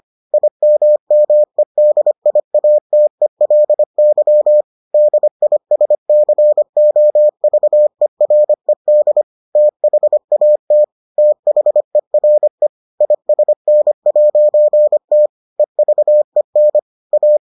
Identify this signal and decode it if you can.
Morse code, heard as a keyed tone, about 25 wpm, 600 Hz. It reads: IMMEDIATELY DISCOVERED THAT THERE ISN'T EVEN A